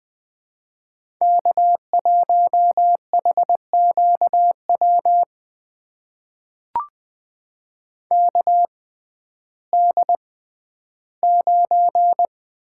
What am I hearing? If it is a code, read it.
Morse code, heard as a keyed tone, about 20 wpm, spaced as for 12 wpm, 700 Hz. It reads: K1HQW KD9
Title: USA Call Signs - 5 Characters - 6x Character Spacing - 20wpm